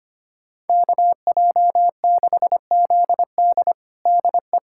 Morse code, heard as a keyed tone, about 25 wpm, 700 Hz. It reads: KJ6ZD DE
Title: USA Callsign DE Callsign - 5 Characters - 25wpm